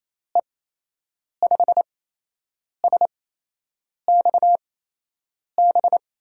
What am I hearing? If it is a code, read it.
Morse code, heard as a keyed tone, about 28 wpm, spaced as for 8 wpm, 700 Hz. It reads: E5SXB
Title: USA Call Signs - 6 Characters - 8x Character Spacing - 28wpm